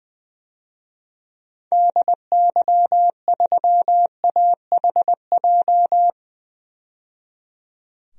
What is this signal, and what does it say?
Morse code, heard as a keyed tone, about 21 wpm, 700 Hz. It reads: DY3AHJ
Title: DXCC-100 Call Signs - 6 Characters - 20wpm